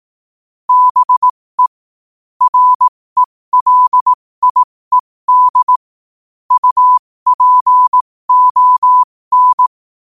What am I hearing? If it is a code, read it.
Morse code, heard as a keyed tone, about 18 wpm, spaced as for 15 wpm, 1000 Hz. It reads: BE RELIED UPON
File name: qotdfiles / 15wpm